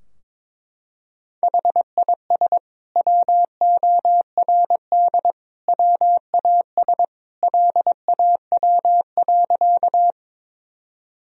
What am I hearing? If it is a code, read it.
Morse code, heard as a keyed tone, about 22 wpm, 700 Hz. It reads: HIS WORD WAS LAW.
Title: Sentences from Top 1000 Words - Review Part 3 - 22wpm